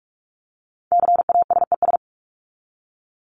Morse code, heard as a keyed tone, about 44 wpm, 700 Hz. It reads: CASES
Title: ES:S Suffix - Top 100 Words with Base Words 45wpm